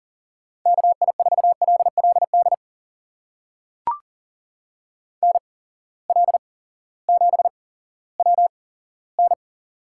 Morse code, heard as a keyed tone, about 40 wpm, spaced as for 18 wpm, 700 Hz. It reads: KI4LLD NL7WN
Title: USA Call Signs - 8x Character Spacing - 40wpm